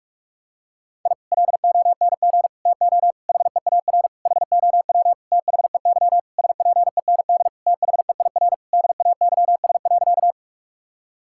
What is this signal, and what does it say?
Morse code, heard as a keyed tone, about 46 wpm, 700 Hz. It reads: I LONG TO HEAR HOW THEY SPEND THEIR DAYS.
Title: Sentences from Top 1000 Words - Review Part 5 - 45wpm